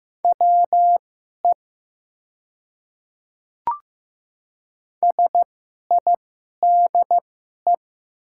Morse code, heard as a keyed tone, about 15 wpm, spaced as for 11 wpm, 700 Hz. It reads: WE SIDE
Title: 2x Character Spacing - Top 200 Words - 15wpm